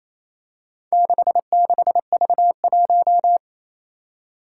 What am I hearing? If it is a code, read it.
Morse code, heard as a keyed tone, about 29 wpm, 700 Hz. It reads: 66V1
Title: Four Challenging Letter-Number Combinations 28wpm